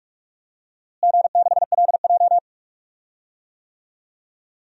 Morse code, heard as a keyed tone, about 42 wpm, 700 Hz. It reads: G6LJ
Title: DXCC-100 Call Signs - 4 Characters - Repeated 3x - 45wpm